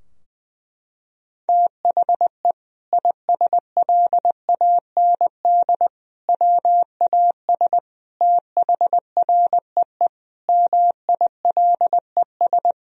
Morse code, heard as a keyed tone, about 20 wpm, 700 Hz. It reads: THE ISLAND WAS THREE MILES
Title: Sentences from Top 1000 Words - Review Part 3 - 20wpm